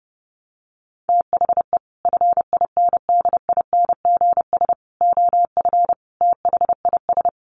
Morse code, heard as a keyed tone, about 30 wpm, 700 Hz. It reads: THE FINDINGS OF THIS
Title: Sets of 5 Words Encore2 30wpm